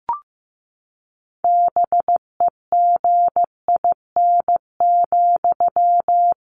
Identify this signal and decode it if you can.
Morse code, heard as a keyed tone, about 15 wpm, 700 Hz. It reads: BEGIN,
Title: Sentences from Top 1500 Words 15wpm